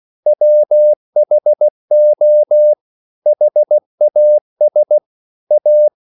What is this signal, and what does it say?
Morse code, heard as a keyed tone, about 16 wpm, 600 Hz. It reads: WHO HAS A
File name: CW / Practice / EnglishQuotations0216